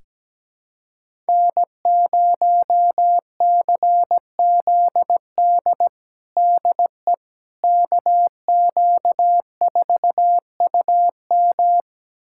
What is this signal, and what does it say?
Morse code, heard as a keyed tone, about 17 wpm, 700 Hz. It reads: N0CZD DE KQ4UM
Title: USA Callsign DE Callsign - 5 Characters - 17wpm